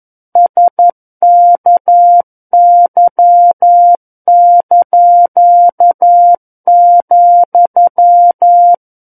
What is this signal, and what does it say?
Morse code, heard as a keyed tone, about 11 wpm, 700 Hz. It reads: SKY),
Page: War of the Worlds - 02-Chapter 2 - 11 WPM